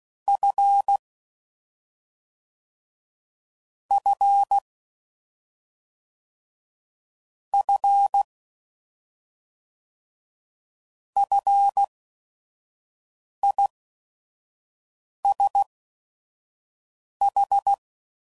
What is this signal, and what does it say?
Morse code, heard as a keyed tone, about 16 wpm, spaced as for 5 wpm, 800 Hz. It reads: F F F FISH
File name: Prefixes / FLs & THs / Single Fs Ls